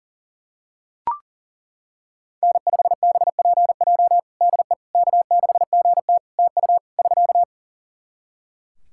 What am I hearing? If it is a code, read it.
Morse code, heard as a keyed tone, about 40 wpm, 700 Hz. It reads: N5BPJ DE K6GT TU <SK>
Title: K1USN SST QSO Practice 40wpm